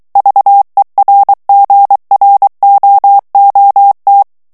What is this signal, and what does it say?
Morse code, heard as a keyed tone, about 23 wpm, 800 Hz. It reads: VERGROOT
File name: HOLLANDAIS texte clair 1 vitesse 22 mots minute